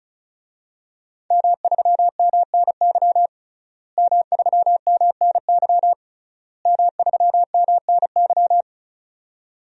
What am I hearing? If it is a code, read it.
Morse code, heard as a keyed tone, about 36 wpm, 700 Hz. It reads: M3MNY M3MNY M3MNY
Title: DXCC-100 Call Signs - 5 Characters - Repeated 3x - 35wpm